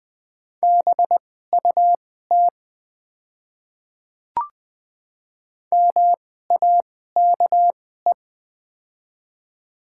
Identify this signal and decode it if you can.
Morse code, heard as a keyed tone, about 20 wpm, spaced as for 14 wpm, 700 Hz. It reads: BUT MAKE
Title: 2x Character Spacing - Top 200 Words - 20wpm